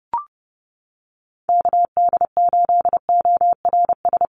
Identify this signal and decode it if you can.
Morse code, heard as a keyed tone, about 30 wpm, 700 Hz. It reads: KD8ORS
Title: USA Call Signs Repeated 2x 30wpm